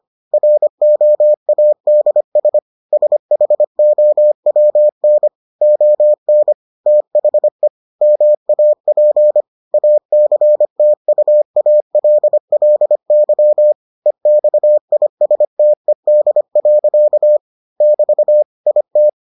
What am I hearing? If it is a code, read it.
Morse code, heard as a keyed tone, about 25 wpm, 600 Hz. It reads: ROADS SHOWN ON THE MAP ACTUALLY EXISTED. <BT> IT